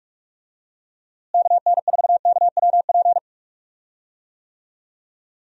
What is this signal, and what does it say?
Morse code, heard as a keyed tone, about 46 wpm, 700 Hz. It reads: KN4KWP
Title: USA Call Signs - 6 Characters - Repeated 2x - 45wpm